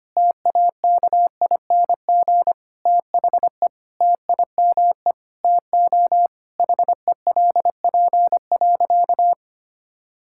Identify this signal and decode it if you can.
Morse code, heard as a keyed tone, about 25 wpm, 700 Hz. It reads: TAKING THE TIME TO HELP.
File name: practice / 25wpm / Sentences with 9-Word Phrases - 25wpm